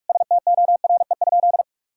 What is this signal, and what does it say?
Morse code, heard as a keyed tone, about 46 wpm, 700 Hz. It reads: STORE?
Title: Sentences from Top 1000 Words - Review Part 6 - 45wpm